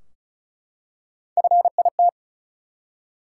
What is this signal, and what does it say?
Morse code, heard as a keyed tone, about 34 wpm, 700 Hz. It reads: FIT
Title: Three Letter Words - 35wpm